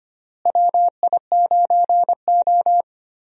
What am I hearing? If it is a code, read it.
Morse code, heard as a keyed tone, about 25 wpm, 700 Hz. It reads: WI9O
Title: USA Call Signs - 4 Characters - Repeated 2x - 25wpm